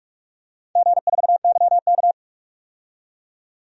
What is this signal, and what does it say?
Morse code, heard as a keyed tone, about 42 wpm, 700 Hz. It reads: G4YK
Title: DXCC-100 Call Signs - 4 Characters - 45wpm